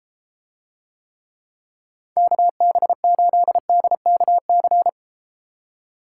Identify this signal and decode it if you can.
Morse code, heard as a keyed tone, about 32 wpm, 700 Hz. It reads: KB8DKC